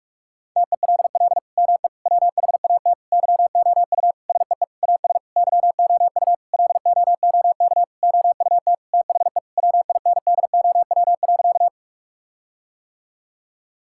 Morse code, heard as a keyed tone, about 46 wpm, 700 Hz. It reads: TELL ME WHAT YOU SEE AS YOU LOOK OUT THE WINDOW.
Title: Sentences from Top 1600 Words 45wpm